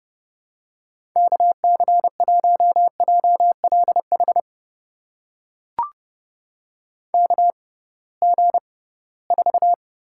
Morse code, heard as a keyed tone, about 30 wpm, spaced as for 19 wpm, 700 Hz. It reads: KC1JLH KG4
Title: USA Call Signs - 6 Characters - 6x Character Spacing - 30wpm